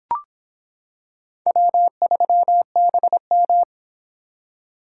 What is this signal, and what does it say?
Morse code, heard as a keyed tone, about 26 wpm, 700 Hz. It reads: W3BM